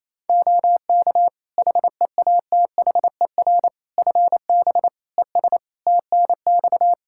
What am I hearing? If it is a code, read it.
Morse code, heard as a keyed tone, about 27 wpm, 700 Hz. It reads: OK HEATHER FB ES TNX
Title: K1USN SST QSO Practice 28wpm